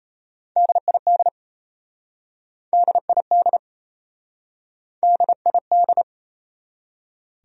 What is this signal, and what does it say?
Morse code, heard as a keyed tone, about 37 wpm, 700 Hz. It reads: DID DID DID